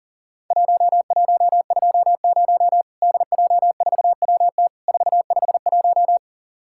Morse code, heard as a keyed tone, about 42 wpm, 700 Hz. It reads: 1120 DJ4WT 451
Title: Worked All Europe DX Contest - QTC Exchange - 40wpm